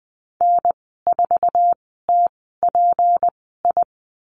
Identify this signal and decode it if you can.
Morse code, heard as a keyed tone, about 20 wpm, spaced as for 15 wpm, 700 Hz. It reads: N4TPI
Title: 154 Lesson 25g - Letter p - Call Signs 20wpm